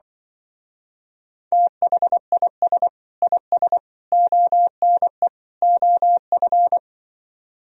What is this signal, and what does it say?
Morse code, heard as a keyed tone, about 25 wpm, 700 Hz. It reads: THIS IS ONE OF